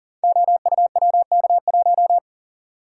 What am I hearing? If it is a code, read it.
Morse code, heard as a keyed tone, about 40 wpm, 700 Hz. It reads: OUWK1
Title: Code Groups - 5 Letter:Numbers - 40wpm